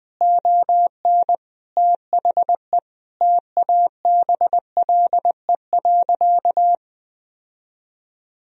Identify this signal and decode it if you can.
Morse code, heard as a keyed tone, about 20 wpm, 700 Hz. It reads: ON THE TABLE.
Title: Sentences from Top 1000 Words - Review Part 4 - 20wpm